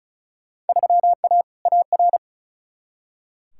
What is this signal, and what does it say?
Morse code, heard as a keyed tone, about 36 wpm, 700 Hz. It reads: 3A AR
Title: Field Day Exchanges - 35wpm